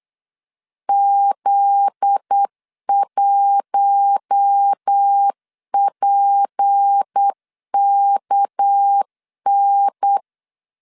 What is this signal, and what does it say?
Morse code, heard as a keyed tone, about 8 wpm, 800 Hz. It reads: Z1PKN
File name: LEZIONI / 9 2 8wpm 3437